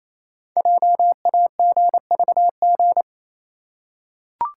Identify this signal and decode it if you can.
Morse code, heard as a keyed tone, about 28 wpm, 700 Hz. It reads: JAGVG
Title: Code Groups - 5 Letters - 28wpm